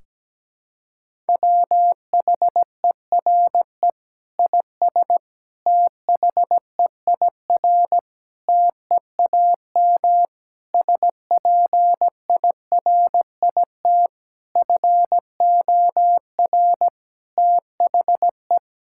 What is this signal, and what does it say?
Morse code, heard as a keyed tone, about 17 wpm, 700 Hz. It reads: WHERE IS THEIR TEAM SPIRIT FOR THE
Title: Sentences from Top 1600 Words 17wpm